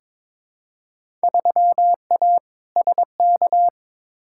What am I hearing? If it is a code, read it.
Morse code, heard as a keyed tone, about 21 wpm, 700 Hz. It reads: 3A SK
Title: Field Day Exchanges Encore - 22wpm